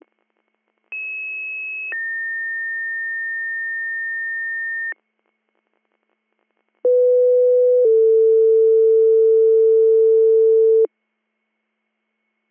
Two-tone page, heard as a one-second tone, then a three-second tone, about 2600 then 1800 Hz, about 500 then 450 Hz. It is heard from 0.9 to 10.9 s.